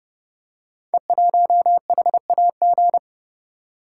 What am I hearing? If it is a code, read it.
Morse code, heard as a keyed tone, about 28 wpm, 700 Hz. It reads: E1HAG
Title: Code Groups - 5 Letter:Numbers - 30wpm